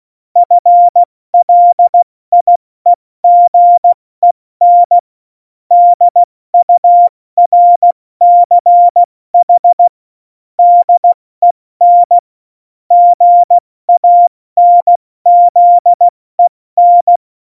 Morse code, heard as a keyed tone, about 16 wpm, 700 Hz. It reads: FLIEGEN DURCH DEN GANZEN
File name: Morsekurs / Witze / Witz 15. Oktober 2019, 70 F80